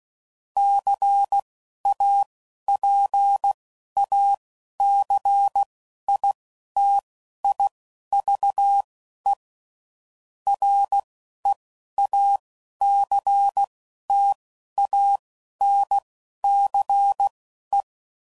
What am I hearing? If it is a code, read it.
Morse code, heard as a keyed tone, about 16 wpm, spaced as for 11 wpm, 800 Hz. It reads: CAPACITIVE REACTANCE
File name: CW Code Groups @10WPM / Technical Subjects / XC